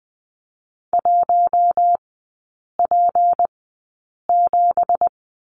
Morse code, heard as a keyed tone, about 20 wpm, spaced as for 8 wpm, 700 Hz. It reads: 1P7
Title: 205 Lesson 33e - Letter j - Three Characters 20wpm